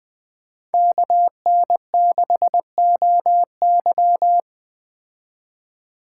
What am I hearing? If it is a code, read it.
Morse code, heard as a keyed tone, about 20 wpm, 700 Hz. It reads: KN6OY
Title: Speed Racing - USA 2x2 Call Signs 17wpm